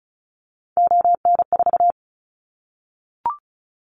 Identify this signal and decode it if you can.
Morse code, heard as a keyed tone, about 36 wpm, 700 Hz. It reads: ON4
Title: Callsign Prefixes for Top 25 DXCC - 35wpm